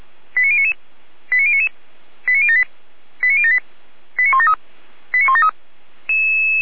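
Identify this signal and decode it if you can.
Five-tone selective call (ZVEI), heard as quick runs of five tones, tones of about 70 ms. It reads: 89099 89099 89078 89078 89173 89173
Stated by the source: ZVEI five-tone selective call received by SDR, fire brigade Feuerwehr Orlamünde.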